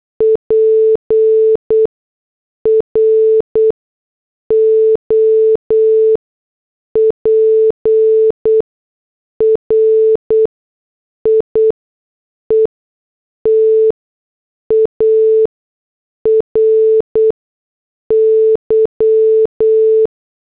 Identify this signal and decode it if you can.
Morse code, heard as a keyed tone, about 8 wpm, spaced as for 6 wpm, 440 Hz. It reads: PROPRIETARY
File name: morse / headlines